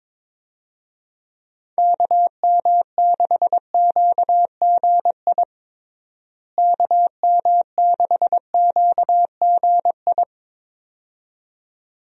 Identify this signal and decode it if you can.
Morse code, heard as a keyed tone, about 22 wpm, 700 Hz. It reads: KM6QGI KM6QGI
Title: USA Call Signs - 6 Characters - Repeated 2x - 22wpm